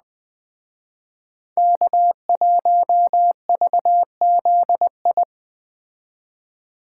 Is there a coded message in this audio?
Yes: Morse code from 1.6 to 5.2 s, heard as a keyed tone, about 20 wpm, 700 Hz. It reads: K1VZI